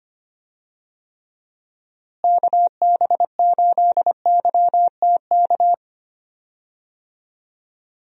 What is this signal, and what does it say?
Morse code, heard as a keyed tone, about 25 wpm, 700 Hz. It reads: KB8YTK
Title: USA Call Signs - 6 Characters - Repeated 2x - 25wpm